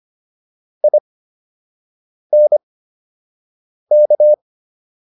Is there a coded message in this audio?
Yes: Morse code from 0.8 to 4.3 s, heard as a keyed tone, about 25 wpm, spaced as for 6 wpm, 600 Hz. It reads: INK